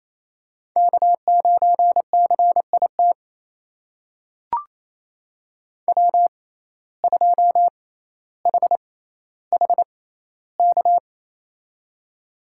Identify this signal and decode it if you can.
Morse code, heard as a keyed tone, about 28 wpm, spaced as for 14 wpm, 700 Hz. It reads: K9CIT W2HHK
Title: USA Call Signs - 6x Character Spacing - 28wpm